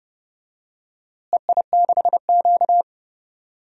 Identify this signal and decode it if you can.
Morse code, heard as a keyed tone, about 31 wpm, 700 Hz. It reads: EI6Q
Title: DXCC-100 Call Signs - 4 Characters - Repeated 3x - 30wpm